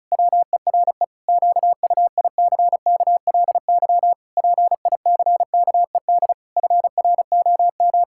Morse code, heard as a keyed tone, about 34 wpm, 700 Hz. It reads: WERE QUICKLY PICKED FROM
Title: Pangram Challenge v2 - 35wpm